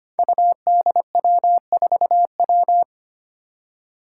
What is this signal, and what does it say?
Morse code, heard as a keyed tone, about 25 wpm, 700 Hz. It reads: UDW4W
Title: Five Challenging Letter-Number Combinations 25wpm